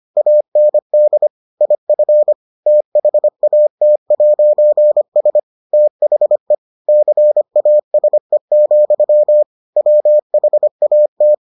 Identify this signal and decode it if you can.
Morse code, heard as a keyed tone, about 25 wpm, 600 Hz. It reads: AND IF THAT'S THE CASE, WHAT